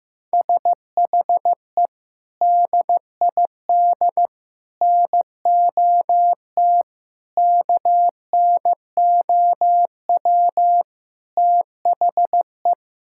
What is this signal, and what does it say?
Morse code, heard as a keyed tone, about 15 wpm, 700 Hz. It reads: SHE DID NOT KNOW THE
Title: Sentences from Top 1600 Words 15wpm